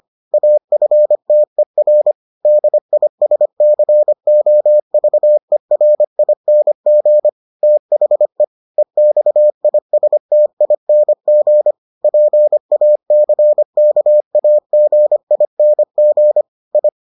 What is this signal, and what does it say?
Morse code, heard as a keyed tone, about 25 wpm, 600 Hz. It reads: AFTER DISCOVERING THE EXISTING PACKAGING I